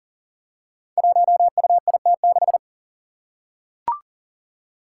Morse code, heard as a keyed tone, about 40 wpm, 700 Hz. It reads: 1UIT6
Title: Five Challenging Letter-Number Combinations 40wpm